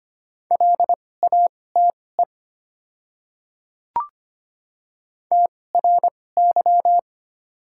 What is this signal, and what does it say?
Morse code, heard as a keyed tone, about 25 wpm, spaced as for 18 wpm, 700 Hz. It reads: LATE TRY
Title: 2x Character Spacing - Top 300 Words - 25wpm